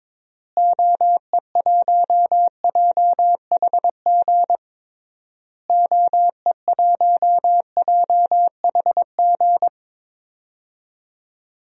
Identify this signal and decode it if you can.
Morse code, heard as a keyed tone, about 22 wpm, 700 Hz. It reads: OE1JHG OE1JHG
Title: DXCC-100 Call Signs - 6 Characters - Repeated 3x - 22wpm